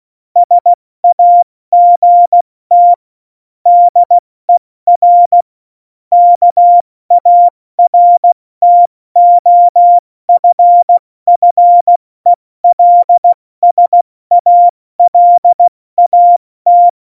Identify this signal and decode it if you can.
Morse code, heard as a keyed tone, about 16 wpm, spaced as for 14 wpm, 700 Hz. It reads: SAGT DER KARTOFFELSALAT